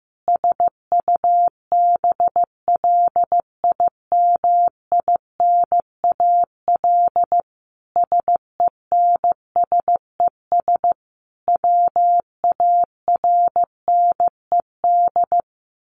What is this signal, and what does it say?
Morse code, heard as a keyed tone, about 15 wpm, 700 Hz. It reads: SUBLIMINAL SENSES WARNED
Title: A Sound Of Thunder 15wpm